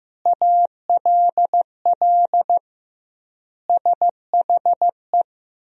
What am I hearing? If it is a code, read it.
Morse code, heard as a keyed tone, about 15 wpm, 700 Hz. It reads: ALL SHE
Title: 2x Word-Spacing - Sentences from Top 100 Words - 15wpm